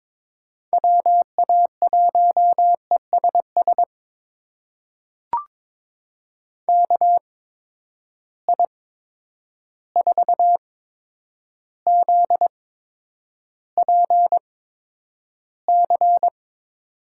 Morse code, heard as a keyed tone, about 22 wpm, spaced as for 9 wpm, 700 Hz. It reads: WA1ESS KI4ZPC
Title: USA Call Signs - 6 Characters - 8x Character Spacing - 22wpm